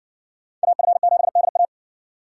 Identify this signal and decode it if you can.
Morse code, heard as a keyed tone, about 58 wpm, 700 Hz. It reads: AF7DA